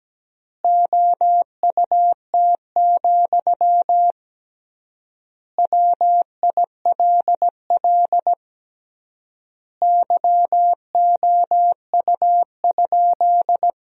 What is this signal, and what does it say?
Morse code, heard as a keyed tone, about 17 wpm, 700 Hz. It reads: OUT, WILL YOU?
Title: 3x Word-Spacing - Sentences from Top 300 Words - 17wpm